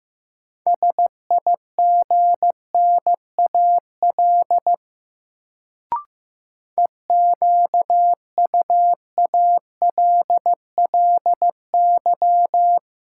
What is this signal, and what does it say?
Morse code, heard as a keyed tone, about 15 wpm, 700 Hz. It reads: SIGNAL EQUALLY
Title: Sentences from Top 1400 Words - New Word Drill - 15wpm